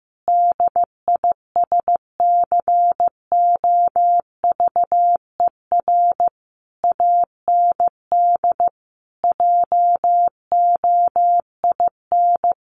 Morse code, heard as a keyed tone, about 15 wpm, 700 Hz. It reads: DISCOVER AND JOIN
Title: Sentences from Top 1400 Words 15wpm